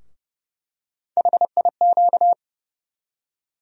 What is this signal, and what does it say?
Morse code, heard as a keyed tone, about 30 wpm, 700 Hz. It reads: HIQ
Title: USA Call Sign Postfixes - 3 Letters 30wpm